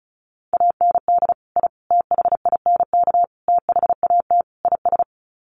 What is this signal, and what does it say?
Morse code, heard as a keyed tone, about 34 wpm, 700 Hz. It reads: AND I THINK THAT IS
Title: Sets of 5 Words Encore2 35wpm